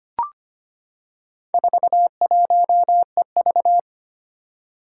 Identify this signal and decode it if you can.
Morse code, heard as a keyed tone, about 25 wpm, 700 Hz. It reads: 41EV
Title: Four Challenging Letter-Number Combinations 25wpm